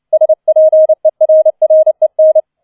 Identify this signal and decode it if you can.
Morse code, heard as a keyed tone, about 29 wpm, 630 Hz. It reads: SPERREN